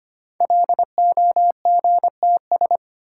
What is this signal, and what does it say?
Morse code, heard as a keyed tone, about 25 wpm, 700 Hz. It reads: LOGTS
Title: Code Groups - 5 Letter:Numbers - 25wpm